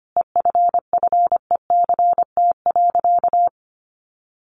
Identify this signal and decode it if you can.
Morse code, heard as a keyed tone, about 25 wpm, 700 Hz. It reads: EFFECT.
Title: Sentences from Top 1400 Words 25wpm